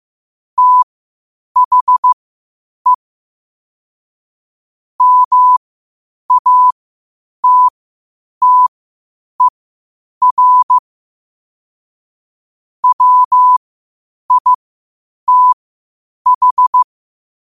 Morse code, heard as a keyed tone, about 15 wpm, spaced as for 8 wpm, 1000 Hz. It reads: THE MATTER WITH
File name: qotdfiles / 8wpm